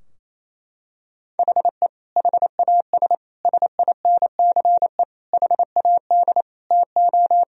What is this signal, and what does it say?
Morse code, heard as a keyed tone, about 28 wpm, 700 Hz. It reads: HE HAS SINCE HAD TO
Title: Sentences from Top 1000 Words - Review Part 2 - 28wpm